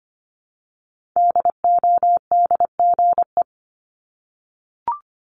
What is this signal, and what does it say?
Morse code, heard as a keyed tone, about 25 wpm, 700 Hz. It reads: DODGE
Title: Multi-letter Phonemes Repeated 3x 25wpm